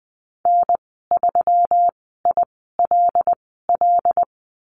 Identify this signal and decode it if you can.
Morse code, heard as a keyed tone, about 20 wpm, spaced as for 15 wpm, 700 Hz. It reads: N3ILL
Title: 122 Lesson 20g - Number 3 - Call Signs 20wpm